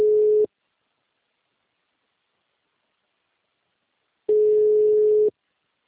Ringback tone (Europe, 425 Hz).